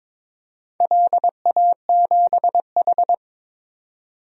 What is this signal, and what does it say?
Morse code, heard as a keyed tone, about 22 wpm, 700 Hz. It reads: LA7H
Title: DXCC-100 Call Signs - 4 Characters - 22wpm